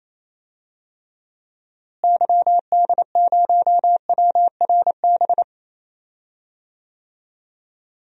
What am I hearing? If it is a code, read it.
Morse code, heard as a keyed tone, about 28 wpm, 700 Hz. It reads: YD0WRB
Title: DXCC-100 Call Signs - 6 Characters - Repeated 3x - 28wpm